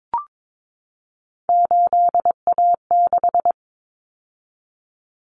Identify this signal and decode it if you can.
Morse code, heard as a keyed tone, about 22 wpm, 700 Hz. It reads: 8A6
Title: Common Cut Numbers - 3-Digit 22wpm